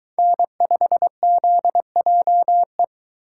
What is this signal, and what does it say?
Morse code, heard as a keyed tone, about 23 wpm, 700 Hz. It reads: N5ZJE